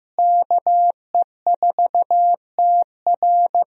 Morse code, heard as a keyed tone, about 15 wpm, 700 Hz. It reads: KE4TR
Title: USA Call Signs - 5 Characters - Repeated 3x - 15wpm